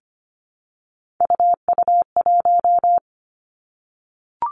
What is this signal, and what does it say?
Morse code, heard as a keyed tone, about 25 wpm, 700 Hz. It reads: UU1
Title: Three Challenging Letter-Number Combinations 25wpm